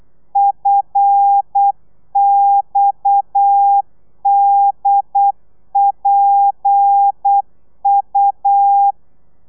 Morse code, heard as a keyed tone, about 8 wpm, 800 Hz. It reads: FXDPU